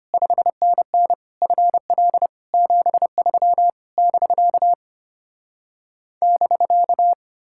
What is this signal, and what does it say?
Morse code, heard as a keyed tone, about 33 wpm, 700 Hz. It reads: 5NN FL 73 <BK> <BK>